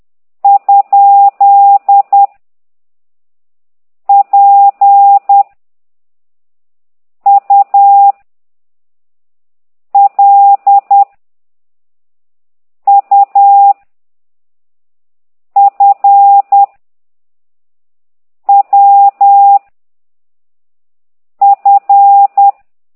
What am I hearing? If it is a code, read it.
Morse code, heard as a keyed tone, about 10 wpm, 800 Hz. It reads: ? P U L U F W F
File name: ct2idw / exerc8